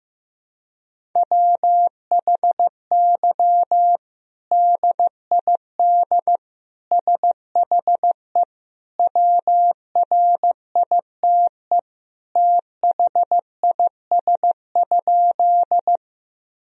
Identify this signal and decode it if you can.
Morse code, heard as a keyed tone, about 15 wpm, 700 Hz. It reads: WHY DID SHE WRITE THIS?
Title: Sentences from Top 1000 Words - Review Part 1 - 15wpm